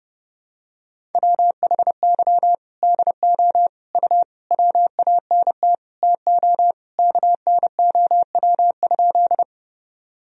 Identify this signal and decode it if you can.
Morse code, heard as a keyed tone, about 30 wpm, 700 Hz. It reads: WHY DO U WANT TO KNOW?
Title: Sentences from Top 200 Words and Abbreviations 30wpm